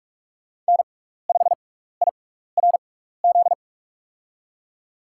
Morse code, heard as a keyed tone, about 41 wpm, spaced as for 15 wpm, 700 Hz. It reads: N5IRZ
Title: USA Call Signs - 6x Character Spacing - 45wpm